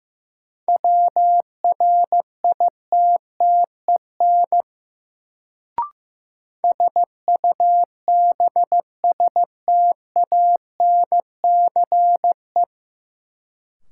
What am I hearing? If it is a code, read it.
Morse code, heard as a keyed tone, about 15 wpm, 700 Hz. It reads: WRITTEN SUBSTANCE
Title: Sentences from Top 1500 Words - New Word Drill Alt - 15wpm